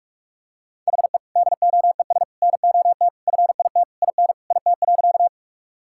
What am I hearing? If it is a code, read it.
Morse code, heard as a keyed tone, about 46 wpm, 700 Hz. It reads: HE DOES NOT FIT IN IT.
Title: Sentences from Top 1000 Words - Review Part 5 - 45wpm